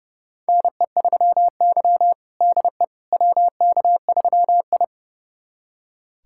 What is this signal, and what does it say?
Morse code, heard as a keyed tone, about 30 wpm, 700 Hz. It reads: NE3Y DE WK3I
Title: USA Callsign DE Callsign - 4 Characters - 30wpm